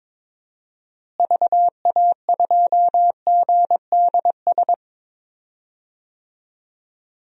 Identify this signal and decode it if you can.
Morse code, heard as a keyed tone, about 22 wpm, 700 Hz. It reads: VA2GDS
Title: DXCC-100 Call Signs - 6 Characters - Repeated 2x - 22wpm